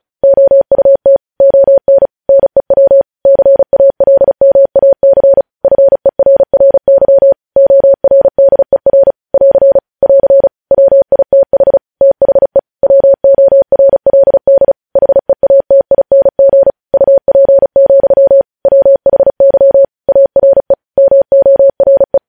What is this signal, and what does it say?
Morse code, heard as a keyed tone, about 36 wpm, 550 Hz. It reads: OUT ON NEW CALMAC FERRY ORDER <AR> <AR> WITH THE WORLD HEATING UP, WHY ARE MORE
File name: podcast / News headlines-1x-35x35